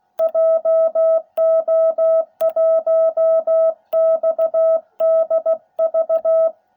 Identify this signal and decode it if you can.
Morse code, heard as a keyed tone, about 16 wpm, 660 Hz. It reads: JO1XDV